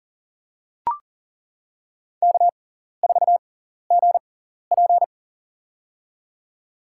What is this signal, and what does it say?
Morse code, heard as a keyed tone, about 37 wpm, spaced as for 14 wpm, 700 Hz. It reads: K4GP
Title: USA Call Signs - 4 Characters - 6x Character Spacing - 40wpm